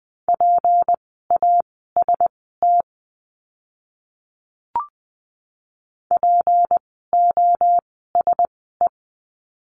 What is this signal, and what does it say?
Morse code, heard as a keyed tone, about 20 wpm, spaced as for 15 wpm, 700 Hz. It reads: PAST POSE
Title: 153 Lesson 25f - Letter p - Words 20wpm